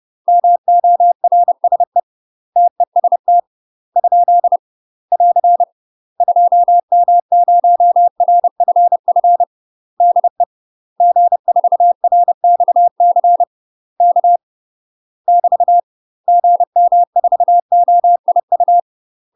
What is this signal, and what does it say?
Morse code, heard as a keyed tone, about 30 wpm, 700 Hz. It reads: MORSE TEST ? <AR> 2M0RFF DE G4RXC K <BT> GM4OIU
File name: UK MASTER QSOs TRAINING / 30WPM / UK MASTER QSOs 0033